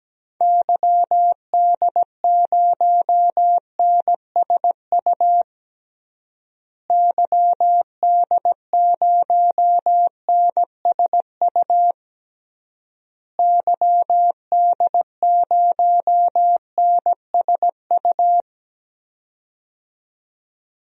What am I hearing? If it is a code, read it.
Morse code, heard as a keyed tone, about 17 wpm, 700 Hz. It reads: YD0NSU YD0NSU YD0NSU
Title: DXCC-100 Call Signs - 6 Characters - Repeated 3x - 17wpm